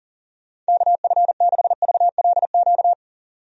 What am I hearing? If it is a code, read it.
Morse code, heard as a keyed tone, about 40 wpm, 700 Hz. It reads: KF6VLQ
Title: USA Call Signs - 6 Characters - 40wpm